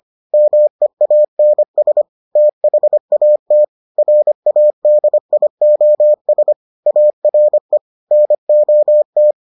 Morse code, heard as a keyed tone, about 25 wpm, 600 Hz. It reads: MEANS THAT RADIOS ARE NOT